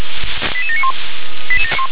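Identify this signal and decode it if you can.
Five-tone selective call (ZVEI), heard as quick runs of five tones, tones of about 70 ms. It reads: 88791
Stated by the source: SDR-received ZVEI five-tone selective call, code 88707.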